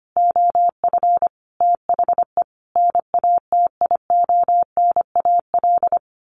Morse code, heard as a keyed tone, about 25 wpm, 700 Hz. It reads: OF THE NATIONAL
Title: Sets of 5 Words Encore2 25wpm